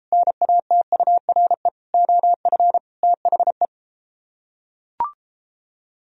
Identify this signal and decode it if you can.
Morse code, heard as a keyed tone, about 31 wpm, 700 Hz. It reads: NATURE OF THE